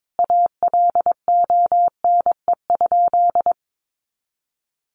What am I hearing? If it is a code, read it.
Morse code, heard as a keyed tone, about 22 wpm, 700 Hz. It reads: ALONE?
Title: Sentences from Top 1000 Words 22wpm